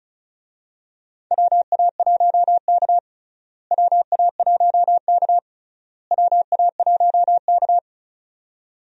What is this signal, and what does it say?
Morse code, heard as a keyed tone, about 34 wpm, 700 Hz. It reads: WA1K WA1K WA1K
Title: USA Call Signs - 4 Characters - Repeated 3x - 35wpm